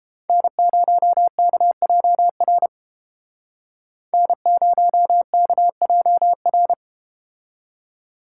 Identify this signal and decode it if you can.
Morse code, heard as a keyed tone, about 32 wpm, 700 Hz. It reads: N0KJR N0KJR